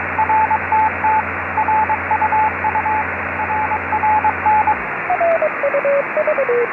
Morse code, heard as a keyed tone, about 22 wpm, 880 Hz, with radio noise. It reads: RTT RUU RRN
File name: xw-2a cw beacon 145mhz662 16h04 utc